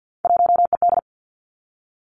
Morse code, heard as a keyed tone, about 50 wpm, 700 Hz. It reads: 1ED
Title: Three Challenging Letter-Number Combinations 50wpm